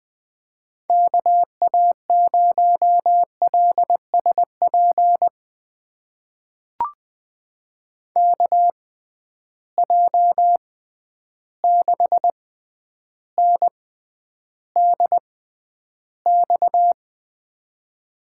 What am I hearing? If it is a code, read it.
Morse code, heard as a keyed tone, about 20 wpm, spaced as for 10 wpm, 700 Hz. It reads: KA0LSP KJ6NDX
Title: USA Call Signs - 6x Character Spacing - 20wpm